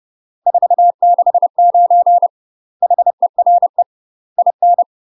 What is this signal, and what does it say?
Morse code, heard as a keyed tone, about 30 wpm, 700 Hz. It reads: 469 HERE IN
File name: UK MASTER QSOs TRAINING / 30WPM / UK MASTER QSOs 0022